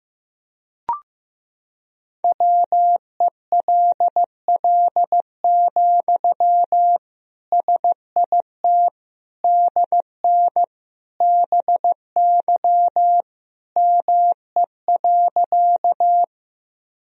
Morse code, heard as a keyed tone, about 15 wpm, 700 Hz. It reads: WELL, SIT DN BY ME.
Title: Sentences from Top 500 Words and Abbreviations 15wpm